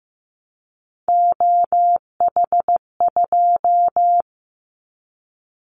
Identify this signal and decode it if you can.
Morse code, heard as a keyed tone, about 15 wpm, 700 Hz. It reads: OH2
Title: Callsign Prefixes for Top 50 DXCC - 15wpm